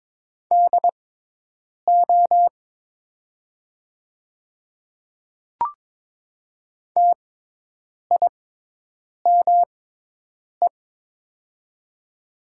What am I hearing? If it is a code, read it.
Morse code, heard as a keyed tone, about 22 wpm, spaced as for 8 wpm, 700 Hz. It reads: DO TIME
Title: 6x Character Spacing - Top 100 Words - 22wpm